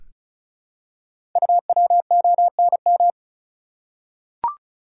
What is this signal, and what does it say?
Morse code, heard as a keyed tone, about 36 wpm, 690 Hz. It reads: UWONM
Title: Code Groups - 5 Letters - 35wpm